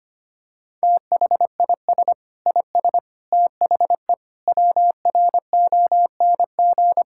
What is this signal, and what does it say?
Morse code, heard as a keyed tone, about 25 wpm, 700 Hz. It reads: THIS IS THE WRONG
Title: Sentences from Top 1000 Words - Review Part 6 - 25wpm